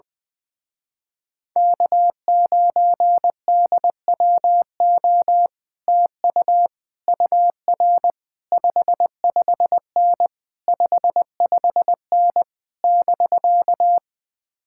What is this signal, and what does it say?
Morse code, heard as a keyed tone, about 20 wpm, 700 Hz. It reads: K9DWO TU UR 55N 55N <BK>
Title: Speed-Racing Common QSO Phrases - 20wpm